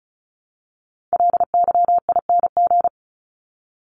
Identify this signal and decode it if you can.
Morse code, heard as a keyed tone, about 34 wpm, 700 Hz. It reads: LYING
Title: ING Suffix - Top 100 Words with Base Words 35wpm